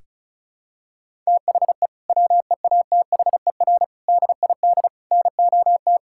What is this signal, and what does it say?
Morse code, heard as a keyed tone, about 36 wpm, 700 Hz. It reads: THE WEATHER DID NOT
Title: Sentences from Top 1000 Words - Review Part 5 - 35wpm